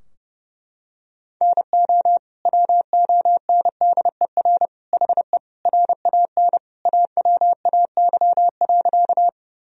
Morse code, heard as a keyed tone, about 30 wpm, 700 Hz. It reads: NO WONDER HE RAN AWAY.
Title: Sentences from Top 1000 Words - Review Part 3 - 30wpm